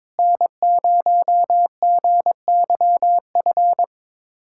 Morse code, heard as a keyed tone, about 22 wpm, 700 Hz. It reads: N0GYF
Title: USA Callsign DE Callsign - 5 Characters - 22wpm